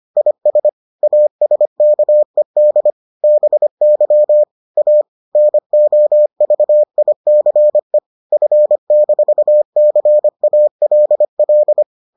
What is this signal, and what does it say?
Morse code, heard as a keyed tone, about 25 wpm, 600 Hz. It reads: IS ASKED BY A NOVICE F-CALL